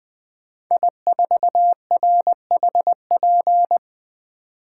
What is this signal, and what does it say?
Morse code, heard as a keyed tone, about 20 wpm, 700 Hz. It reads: I4RHP
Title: 2023 POTA Hunter Call Signs - 20wpm